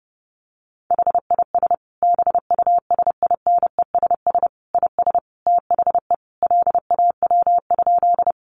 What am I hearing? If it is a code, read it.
Morse code, heard as a keyed tone, about 30 wpm, 700 Hz. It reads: HIS BUSINESS IS THE LAW?
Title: Sentences from Top 600 Words 30wpm